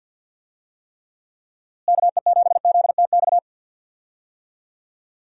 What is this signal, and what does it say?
Morse code, heard as a keyed tone, about 52 wpm, 700 Hz. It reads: KE7ZTX
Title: USA Call Signs - 6 Characters - Repeated 3x - 50wpm